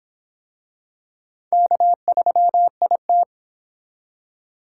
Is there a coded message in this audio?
Yes: Morse code from 1.5 to 3.2 s, heard as a keyed tone, about 24 wpm, 700 Hz. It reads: K3IT